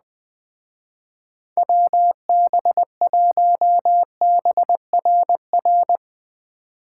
Morse code, heard as a keyed tone, about 20 wpm, 700 Hz. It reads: WB1BRR